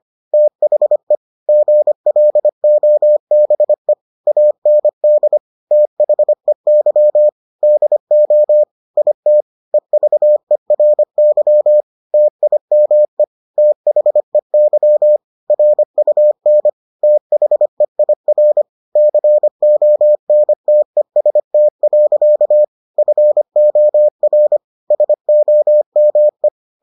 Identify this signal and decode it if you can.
Morse code, heard as a keyed tone, about 25 wpm, 600 Hz. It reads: THE GLOBE AND THEY DO IT EVERY TIME THEY RUN THEIR CONTEST. FOR SOME